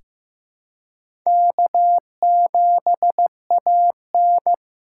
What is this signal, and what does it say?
Morse code, heard as a keyed tone, about 16 wpm, 700 Hz. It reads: K7AN
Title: USA Call Signs - 2x Character Spacing - 15wpm